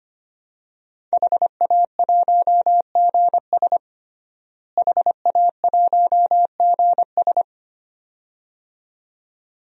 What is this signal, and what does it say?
Morse code, heard as a keyed tone, about 25 wpm, 700 Hz. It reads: HA1GS HA1GS
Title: DXCC-100 Call Signs - 5 Characters - Repeated 2x - 25wpm